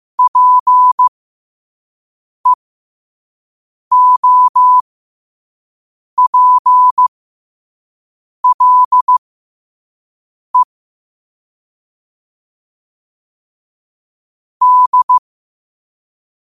Morse code, heard as a keyed tone, about 15 wpm, spaced as for 5 wpm, 1000 Hz. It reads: PEOPLE D